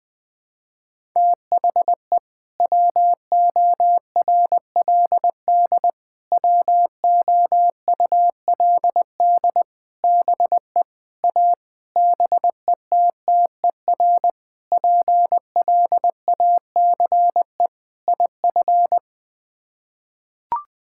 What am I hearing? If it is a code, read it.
Morse code, heard as a keyed tone, about 19 wpm, 700 Hz. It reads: THE WORLD WOULD BE A BETTER PLACE IF
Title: Sets of 8 Words - 20wpm